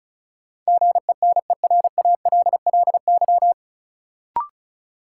Morse code, heard as a keyed tone, about 36 wpm, 700 Hz. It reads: GENERALLY
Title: Sentences from Top 600 Words - New Word Drill - 35wpm